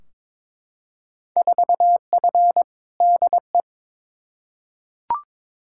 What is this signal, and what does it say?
Morse code, heard as a keyed tone, about 21 wpm, 700 Hz. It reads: 4F DE
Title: Field Day Exchanges Encore - 22wpm